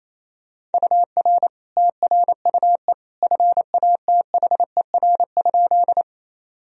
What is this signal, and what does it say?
Morse code, heard as a keyed tone, about 27 wpm, 700 Hz. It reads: UR TRUE FATHER?
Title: Sentences from Top 400 Words and Abbreviations 28wpm